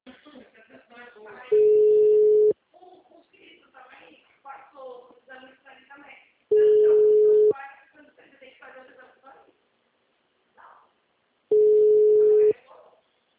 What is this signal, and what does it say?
Ringback tone (Europe, 425 Hz), with speech.